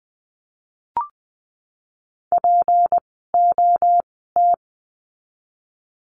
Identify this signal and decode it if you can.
Morse code, heard as a keyed tone, about 20 wpm, spaced as for 15 wpm, 700 Hz. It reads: POT
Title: 153 Lesson 25f - Letter p - Words 20wpm